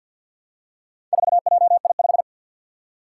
Morse code, heard as a keyed tone, about 52 wpm, 700 Hz. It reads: 4JI5
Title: Four Challenging Letter-Number Combinations 50wpm